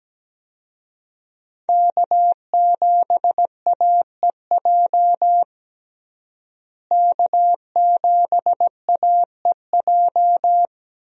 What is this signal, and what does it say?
Morse code, heard as a keyed tone, about 17 wpm, 700 Hz. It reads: K7AEJ K7AEJ
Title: USA Call Signs - 5 Characters - Repeated 2x - 17wpm